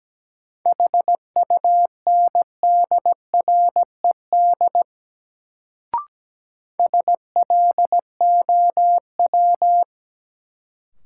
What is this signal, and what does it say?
Morse code, heard as a keyed tone, about 17 wpm, 700 Hz. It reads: HUNDRED SLOW
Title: Sentences from Top 400 Words - New Word Drill - 17wpm